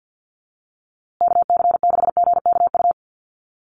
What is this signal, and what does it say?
Morse code, heard as a keyed tone, about 50 wpm, 700 Hz. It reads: KC6GKU
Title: USA Call Signs Repeated 3x 50wpm